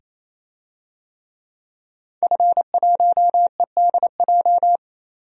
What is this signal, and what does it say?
Morse code, heard as a keyed tone, about 27 wpm, 700 Hz. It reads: F1EDJ
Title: DXCC-100 Call Signs - 5 Characters - 28wpm